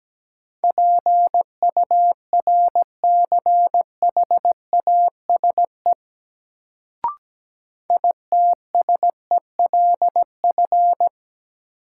Morse code, heard as a keyed tone, about 17 wpm, 700 Hz. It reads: PURCHASE ITSELF
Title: Sentences from Top 800 Words - New Word Drill - 17wpm